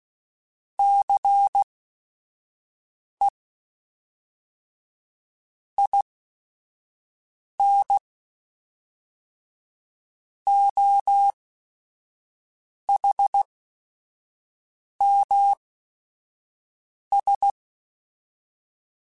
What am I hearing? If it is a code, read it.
Morse code, heard as a keyed tone, about 16 wpm, spaced as for 5 wpm, 800 Hz. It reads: CE IN OHMS
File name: Technical Subjects / Technical Subjects-mp3s / XC @5